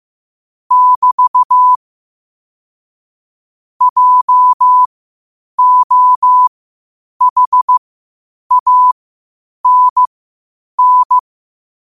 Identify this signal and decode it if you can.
Morse code, heard as a keyed tone, about 15 wpm, spaced as for 8 wpm, 1000 Hz. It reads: <BT> JOHANN
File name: qotdfiles / 8wpm